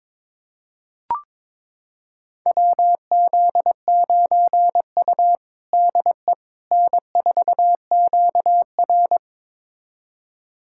Morse code, heard as a keyed tone, about 22 wpm, 700 Hz. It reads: WZ9U DE N4QR
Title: USA Callsign DE Callsign - 4 Characters - 22wpm